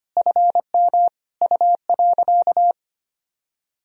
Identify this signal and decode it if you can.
Morse code, heard as a keyed tone, about 25 wpm, 700 Hz. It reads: FM U.
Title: Sentences from Top 100 Words and Abbreviations 25wpm